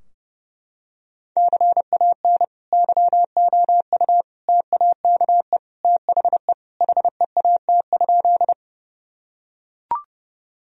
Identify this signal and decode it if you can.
Morse code, heard as a keyed tone, about 30 wpm, 700 Hz. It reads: CAN YOU TAKE THE HEAT?
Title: Sentences from Top 1000 Words - Review Part 3 - 30wpm